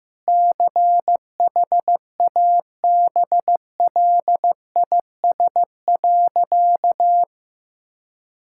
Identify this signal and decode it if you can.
Morse code, heard as a keyed tone, about 15 wpm, 700 Hz. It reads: CHABLIS.
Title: Pangram Challenge v2 - 15wpm